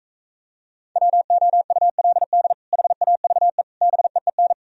Morse code, heard as a keyed tone, about 42 wpm, 700 Hz. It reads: WOULD HAVE BEEN